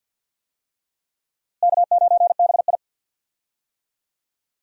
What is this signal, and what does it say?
Morse code, heard as a keyed tone, about 52 wpm, 700 Hz. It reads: K9BI
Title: USA Call Signs - 4 Characters - Repeated 2x - 50wpm